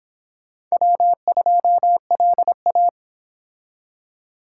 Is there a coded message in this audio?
Yes: Morse code from 0.7 to 2.9 s, heard as a keyed tone, about 25 wpm, 700 Hz. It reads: W2LA